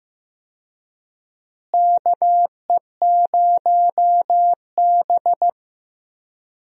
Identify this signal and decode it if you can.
Morse code, heard as a keyed tone, about 15 wpm, 700 Hz. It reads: KE0B
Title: USA Call Signs - 4 Characters - Repeated 3x - 15wpm